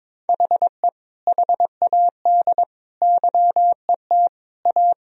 Morse code, heard as a keyed tone, about 22 wpm, 700 Hz. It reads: HE HAD YET A
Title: Sentences from Top 1000 Words - Review Part 2 - 22wpm